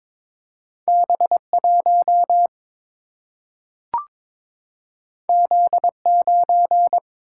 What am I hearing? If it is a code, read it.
Morse code, heard as a keyed tone, about 22 wpm, 700 Hz. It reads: B 1 Z 9
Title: Single-Letter Single-Number 22wpm